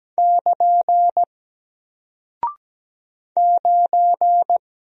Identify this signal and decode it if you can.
Morse code, heard as a keyed tone, about 17 wpm, 700 Hz. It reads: <KN> 9
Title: Single Character (Letter:Number:Punctuation:Prosign) Alt 17wpm